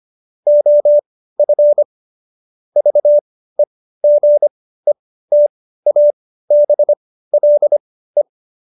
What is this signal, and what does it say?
Morse code, heard as a keyed tone, about 25 wpm, spaced as for 15 wpm, 590 Hz. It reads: OF VEGETABLE
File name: Jokes-25-char-15-eff